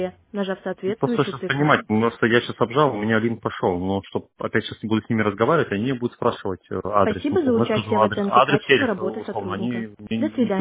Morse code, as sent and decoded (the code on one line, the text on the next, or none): none